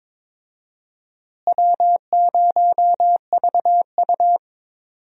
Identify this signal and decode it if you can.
Morse code, heard as a keyed tone, about 22 wpm, 700 Hz. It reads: W0VU
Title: USA Call Signs - 4 Characters - Repeated 2x - 22wpm